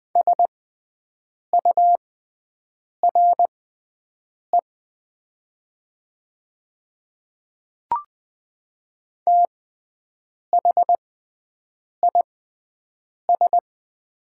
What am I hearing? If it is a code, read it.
Morse code, heard as a keyed tone, about 21 wpm, spaced as for 7 wpm, 700 Hz. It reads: SURE THIS
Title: 6x Character Spacing - Top 300 Words - 20wpm